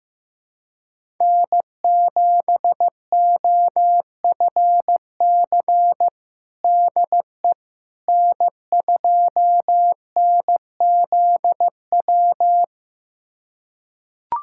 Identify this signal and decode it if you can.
Morse code, heard as a keyed tone, about 15 wpm, 700 Hz. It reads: N7OFC DE N2NZW
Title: USA Callsign DE Callsign - 5 Characters - 15wpm